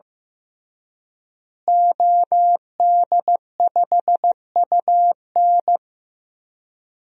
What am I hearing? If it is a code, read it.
Morse code, heard as a keyed tone, about 15 wpm, 700 Hz. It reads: OD5UN
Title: DXCC-100 Call Signs - 5 Characters - Repeated 3x - 15wpm